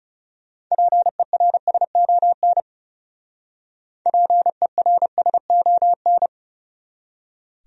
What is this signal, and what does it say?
Morse code, heard as a keyed tone, about 38 wpm, 700 Hz. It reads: PERSON PERSON